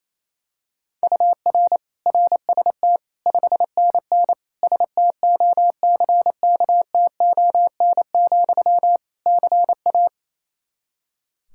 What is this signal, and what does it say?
Morse code, heard as a keyed tone, about 28 wpm, 700 Hz. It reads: UR RST 5NN STOCKTON, CA
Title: K1USN SST QSO Practice 28wpm